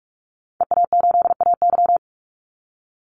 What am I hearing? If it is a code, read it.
Morse code, heard as a keyed tone, about 44 wpm, 700 Hz. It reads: EA8AY
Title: Call Signs - 45wpm